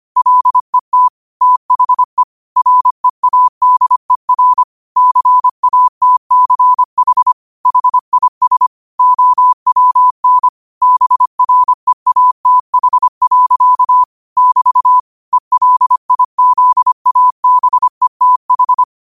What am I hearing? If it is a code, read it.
Morse code, heard as a keyed tone, about 25 wpm, 1000 Hz. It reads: LET THE READER CATCH HIS OWN BREATH. <BT> ELIZABETH